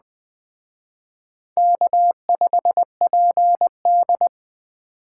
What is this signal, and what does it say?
Morse code, heard as a keyed tone, about 21 wpm, 700 Hz. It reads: K5PD
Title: Speed Racing - USA Call Signs 17wpm